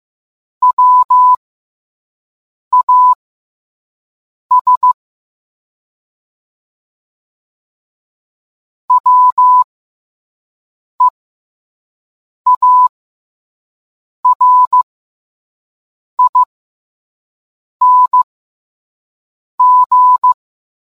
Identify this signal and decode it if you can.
Morse code, heard as a keyed tone, about 15 wpm, spaced as for 5 wpm, 1000 Hz. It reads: WAS WEARING